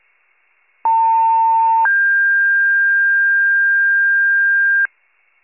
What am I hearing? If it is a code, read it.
Two-tone page, heard as a one-second tone, then a three-second tone, about 900 then 1600 Hz, with noise. It is heard from 0.8 to 4.9 s.